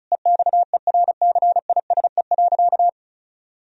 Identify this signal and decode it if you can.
Morse code, heard as a keyed tone, about 34 wpm, 700 Hz. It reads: EXERCISE.
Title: Sentences from Top 1000 Words - Review Part 4 - 35wpm